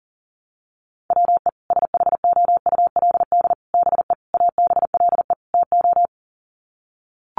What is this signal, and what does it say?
Morse code, heard as a keyed tone, about 40 wpm, 700 Hz. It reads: WE SHOULD BE ABLE TO
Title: Sets of 5 Words Encore2 40wpm